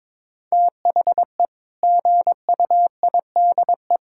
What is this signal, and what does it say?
Morse code, heard as a keyed tone, about 22 wpm, 700 Hz. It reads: THE GUIDE
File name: practice / 22wpm / Sentences with 8-Word Phrases - 22wpm